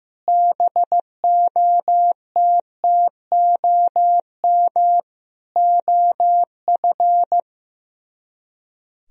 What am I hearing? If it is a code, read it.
Morse code, heard as a keyed tone, about 15 wpm, 700 Hz. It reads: BOTTOM OF
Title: Speed-Racing - Sets of 3 Words 15wpm